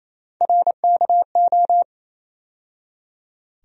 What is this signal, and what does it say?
Morse code, heard as a keyed tone, about 28 wpm, 700 Hz. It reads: RKO
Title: USA Call Sign Postfixes - 3 Letters 28wpm